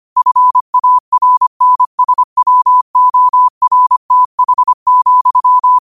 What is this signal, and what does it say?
Morse code, heard as a keyed tone, about 25 wpm, 1000 Hz. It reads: FARNSWORTH,
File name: qotdfiles / 25wpm